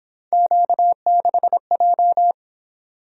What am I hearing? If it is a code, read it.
Morse code, heard as a keyed tone, about 26 wpm, 700 Hz. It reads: Q6J
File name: Speed-Racing - Three Letter-Number Combinations - 22wpm